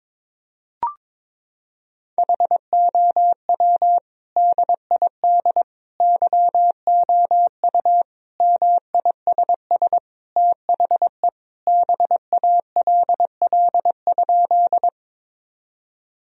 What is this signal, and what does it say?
Morse code, heard as a keyed tone, about 22 wpm, 700 Hz. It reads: HOW DID YOU MISS THE BALL?
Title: Sentences from Top 1000 Words - Review Part 3 - 22wpm